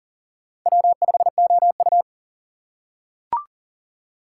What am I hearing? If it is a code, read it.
Morse code, heard as a keyed tone, about 40 wpm, 700 Hz. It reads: W5OU
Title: USA Call Signs - 3x Character Spacing - 40wpm